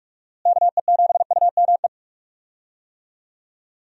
Morse code, heard as a keyed tone, about 46 wpm, 700 Hz. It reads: KE7UME
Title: USA Call Signs - 6 Characters - Repeated 2x - 45wpm